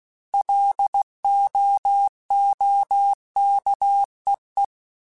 Morse code, heard as a keyed tone, about 16 wpm, 800 Hz. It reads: LOOKEE